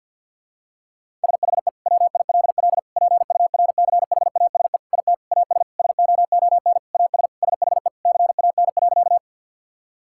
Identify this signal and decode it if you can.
Morse code, heard as a keyed tone, about 48 wpm, 690 Hz. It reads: SHE WILL PURCHASE IT AS SOON AS SHE CAN.
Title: Sentences from Top 1000 Words - Review Part 5 - 50wpm